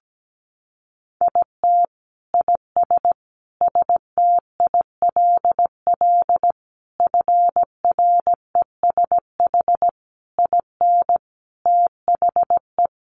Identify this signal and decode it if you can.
Morse code, heard as a keyed tone, about 17 wpm, 700 Hz. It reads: IT IS STILL FRESH IN THE
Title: Sentences from Top 1300 Words 17wpm